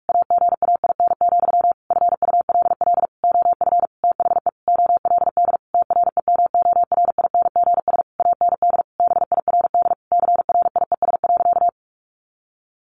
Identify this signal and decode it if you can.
Morse code, heard as a keyed tone, about 44 wpm, 700 Hz. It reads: AGAIN, FULL OF THE OLD TREMORINGS AND BIRD CRIES.
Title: A Sound Of Thunder 45wpm